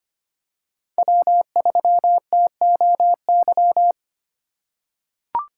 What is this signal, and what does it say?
Morse code, heard as a keyed tone, about 25 wpm, 700 Hz. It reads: W3TOY
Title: USA Call Signs - 8x Character Spacing - 25wpm